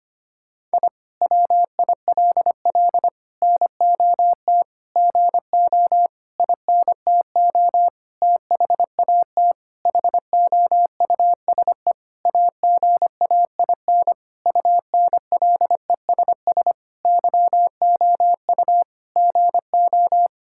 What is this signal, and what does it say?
Morse code, heard as a keyed tone, about 25 wpm, 700 Hz. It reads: I WILL NOT GO INTO THAT HOUSE AGAIN UNLESS YOU GO
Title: Sentences from Top 1000 Words - Review Part 6 - 25wpm